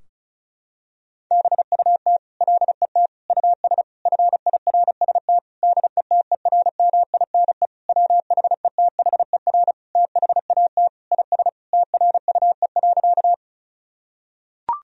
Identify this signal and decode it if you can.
Morse code, heard as a keyed tone, about 34 wpm, 700 Hz. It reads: BUT LET US FIRST DETERMINE WHETHER THAT IS TRUE.
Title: Sentences from Top 1000 Words - Review Part 4 - 35wpm